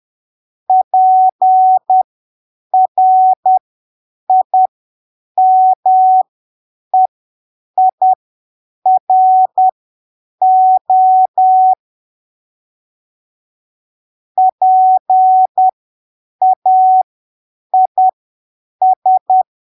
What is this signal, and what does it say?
Morse code, heard as a keyed tone, about 10 wpm, spaced as for 7 wpm, 750 Hz. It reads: PRIMEIRO PAIS
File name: arquivos / texto / texto 16 10ppm